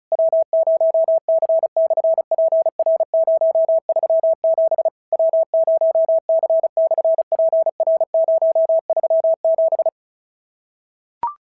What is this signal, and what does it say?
Morse code, heard as a keyed tone, about 33 wpm, 650 Hz. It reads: W0C/PR037 W0C/PR037
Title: SOTA QSO Practice 35wpm